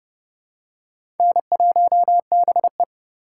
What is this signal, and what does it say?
Morse code, heard as a keyed tone, about 31 wpm, 700 Hz. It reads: N1BE
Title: USA Call Signs - 4 Characters - 6x Character Spacing - 30wpm